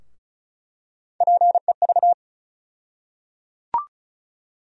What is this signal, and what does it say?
Morse code, heard as a keyed tone, about 36 wpm, 700 Hz. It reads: PEV
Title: USA Call Sign Postfixes - 3 Letters 35wpm